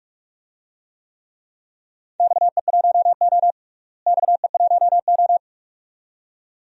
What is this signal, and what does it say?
Morse code, heard as a keyed tone, about 43 wpm, 700 Hz. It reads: XE1O XE1O
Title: DXCC-100 Call Signs - 4 Characters - Repeated 2x - 45wpm